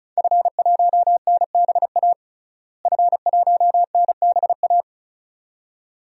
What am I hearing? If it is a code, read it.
Morse code, heard as a keyed tone, about 36 wpm, 700 Hz. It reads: F1NBA F1NBA
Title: DXCC-100 Call Signs - 5 Characters - Repeated 2x - 35wpm